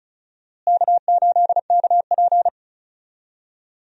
Morse code, heard as a keyed tone, about 36 wpm, 700 Hz. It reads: K8KP
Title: USA Call Signs - 4 Characters - Repeated 3x - 35wpm